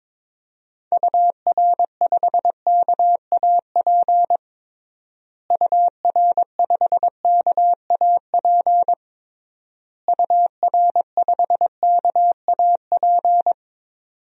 Morse code, heard as a keyed tone, about 22 wpm, 700 Hz. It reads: UR5KAP UR5KAP UR5KAP
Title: DXCC-100 Call Signs - 6 Characters - Repeated 3x - 22wpm